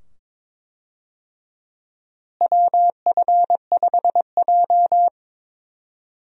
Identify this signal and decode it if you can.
Morse code, heard as a keyed tone, about 23 wpm, 700 Hz. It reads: WF5J
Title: USA Call Signs - 4 Characters - Repeated 2x - 22wpm